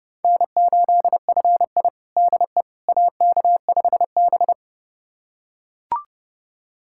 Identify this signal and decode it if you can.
Morse code, heard as a keyed tone, about 32 wpm, 700 Hz. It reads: N8FI DE AK5B
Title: USA Callsign DE Callsign - 4 Characters - 30wpm